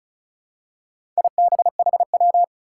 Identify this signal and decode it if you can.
Morse code, heard as a keyed tone, about 36 wpm, 700 Hz. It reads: IBHW
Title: Four Challenging Letter-Number Combinations 35wpm